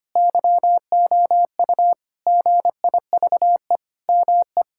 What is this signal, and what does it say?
Morse code, heard as a keyed tone, about 25 wpm, 700 Hz. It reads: YOU GIVE ME
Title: Sentences from Top 1000 Words - Review Part 6 - 25wpm